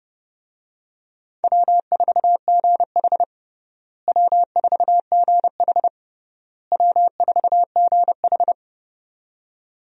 Morse code, heard as a keyed tone, about 32 wpm, 700 Hz. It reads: W4GH W4GH W4GH
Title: USA Call Signs - 4 Characters - Repeated 3x - 30wpm